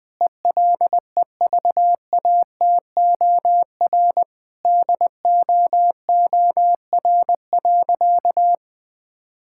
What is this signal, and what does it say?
Morse code, heard as a keyed tone, about 20 wpm, 700 Hz. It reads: ELEVATOR DOOR.
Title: Sentences from Top 1600 Words 20wpm